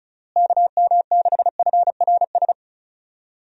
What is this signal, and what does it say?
Morse code, heard as a keyed tone, about 34 wpm, 700 Hz. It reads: KM6FRS
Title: USA Call Signs - 6 Characters - Repeated 3x - 35wpm